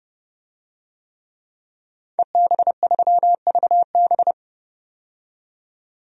Morse code, heard as a keyed tone, about 30 wpm, 700 Hz. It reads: EB3VB
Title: DXCC-100 Call Signs - 5 Characters - 30wpm